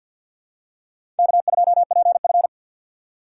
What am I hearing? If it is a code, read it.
Morse code, heard as a keyed tone, about 51 wpm, 700 Hz. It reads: K2PF
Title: USA Call Signs - 4 Characters - 2x Character Spacing - 50wpm